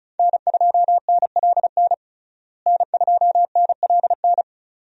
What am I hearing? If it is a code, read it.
Morse code, heard as a keyed tone, about 34 wpm, 700 Hz. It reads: N2NLN N2NLN
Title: USA Call Signs - 5 Characters - Repeated 2x - 35wpm